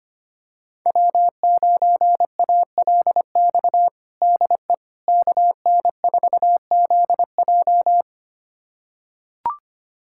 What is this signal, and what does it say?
Morse code, heard as a keyed tone, about 25 wpm, 700 Hz. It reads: W9ALX DE KN4ZJ
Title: USA Callsign DE Callsign - 5 Characters - 25wpm